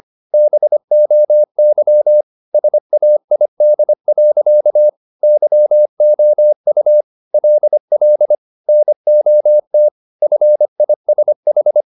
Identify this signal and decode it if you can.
Morse code, heard as a keyed tone, about 25 wpm, 600 Hz. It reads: BOY SAID. YOU LL NOT FISH
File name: the old man and the sea part 1 25wpm